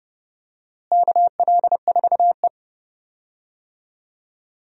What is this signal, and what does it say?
Morse code, heard as a keyed tone, about 30 wpm, 700 Hz. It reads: KL4E
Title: USA Call Signs - 4 Characters - Repeated 2x - 30wpm